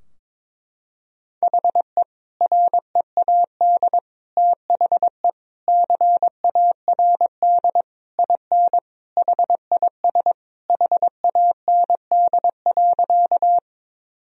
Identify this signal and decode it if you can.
Morse code, heard as a keyed tone, about 22 wpm, 700 Hz. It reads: HE READ THE CARD IN HIS HAND.
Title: Sentences from Top 1000 Words - Review Part 4 - 22wpm